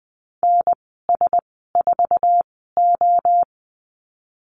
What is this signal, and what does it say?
Morse code, heard as a keyed tone, about 20 wpm, spaced as for 14 wpm, 700 Hz. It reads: NS4O
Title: 055 Lesson 10g - Letter r - Call Signs 20wpm